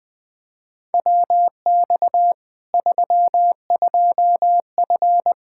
Morse code, heard as a keyed tone, about 20 wpm, 700 Hz. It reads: WX 32F
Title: Practice QSO - Exchange 2 - Info - 20wpm